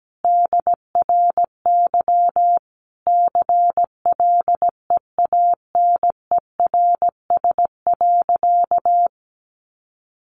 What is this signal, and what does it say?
Morse code, heard as a keyed tone, about 17 wpm, 700 Hz. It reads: DRY CLEANERS.
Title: Additional Easy Phrases 17wpm